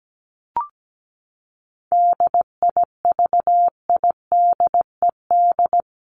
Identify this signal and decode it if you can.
Morse code, heard as a keyed tone, about 17 wpm, 700 Hz. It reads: DIVIDED
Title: ED Suffix - Top 100 Words with Base Words 17wpm